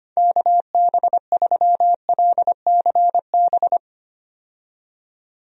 Morse code, heard as a keyed tone, about 25 wpm, 700 Hz. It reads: KB3LCB
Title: USA Call Signs - 6 Characters - 25wpm